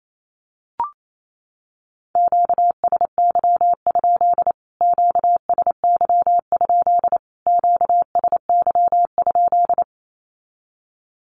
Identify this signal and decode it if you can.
Morse code, heard as a keyed tone, about 28 wpm, 700 Hz. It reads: QSY? QSY? QSY?
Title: Q-Codes Repeated 3x 28wpm